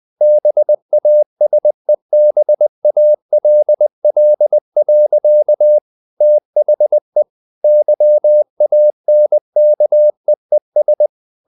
Morse code, heard as a keyed tone, about 20 wpm, 600 Hz. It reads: BASEBALL. THE YANKEES
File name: the old man and the sea / the old man and the sea part 1 20wpm